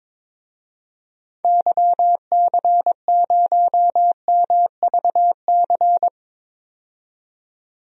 Morse code, heard as a keyed tone, about 22 wpm, 700 Hz. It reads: YC0MVC
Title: DXCC-100 Call Signs - 6 Characters - 22wpm